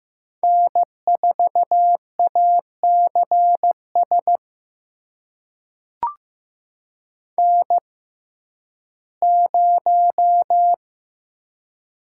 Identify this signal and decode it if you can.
Morse code, heard as a keyed tone, about 15 wpm, spaced as for 11 wpm, 700 Hz. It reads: N4ACS N0
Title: USA Call Signs - 6x Character Spacing - 15wpm